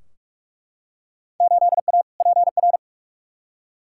Morse code, heard as a keyed tone, about 42 wpm, 700 Hz. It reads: 8A PR
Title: Field Day Exchanges Encore - 45wpm